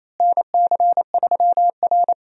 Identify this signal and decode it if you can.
Morse code, heard as a keyed tone, about 28 wpm, 700 Hz. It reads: NC3R
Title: USA Call Signs - 4 Characters - Repeated 2x - 28wpm